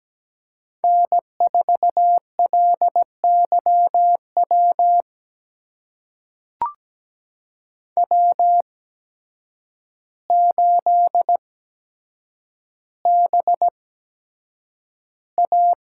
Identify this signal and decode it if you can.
Morse code, heard as a keyed tone, about 17 wpm, spaced as for 8 wpm, 700 Hz. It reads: N4LYW W8BA
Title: USA Call Signs - 5 Characters - 8x Character Spacing - 17wpm